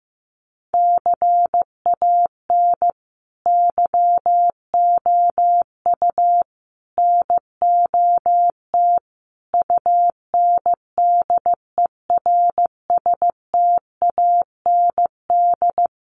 Morse code, heard as a keyed tone, about 15 wpm, 700 Hz. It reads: CAN YOU NOT UNDERSTAND
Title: Sentences from Top 600 Words 15wpm